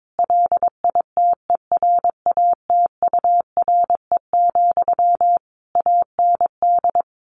Morse code, heard as a keyed tone, about 22 wpm, 700 Hz. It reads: LITERATURE, AND
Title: Sentences from Top 600 Words 22wpm